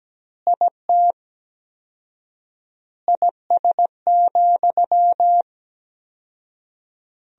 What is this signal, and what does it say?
Morse code, heard as a keyed tone, about 17 wpm, 700 Hz. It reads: IT IS,
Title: 4x Word-Spacing - Sentences from Top 200 Words - 17wpm